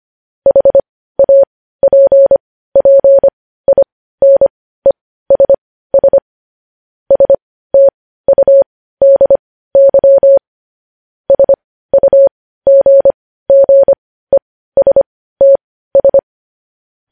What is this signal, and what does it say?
Morse code, heard as a keyed tone, about 25 wpm, spaced as for 15 wpm, 550 Hz. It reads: HAPPINESS STUDY SUGGESTS
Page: news-headlines-in-morse-code-at-15-wpm